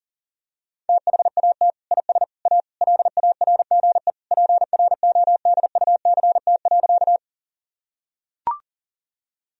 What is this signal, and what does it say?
Morse code, heard as a keyed tone, about 40 wpm, 700 Hz. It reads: THAT IS A LARGE PRODUCT.
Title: Sentences from Top 1000 Words - Review Part 2 - 40wpm